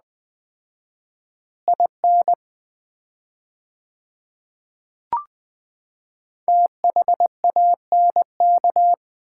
Morse code, heard as a keyed tone, about 20 wpm, 700 Hz. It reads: IN THANK
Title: 4x Word-Spacing - Sets of 4 Words - 20wpm